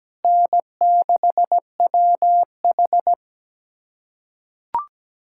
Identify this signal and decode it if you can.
Morse code, heard as a keyed tone, about 17 wpm, 700 Hz. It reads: N6WH
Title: USA Callsign DE Callsign - 4 Characters - 17wpm